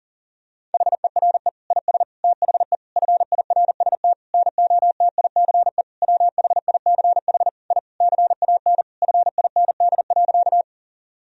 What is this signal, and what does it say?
Morse code, heard as a keyed tone, about 40 wpm, 700 Hz. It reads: HERE IS THE FIRST NOTICE WHICH I CAN FIND.
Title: Sentences from Top 1000 Words - Review Part 2 - 40wpm